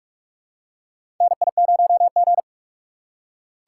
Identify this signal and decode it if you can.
Morse code, heard as a keyed tone, about 44 wpm, 700 Hz. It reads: NI0G
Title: USA Call Signs - 6x Character Spacing - 45wpm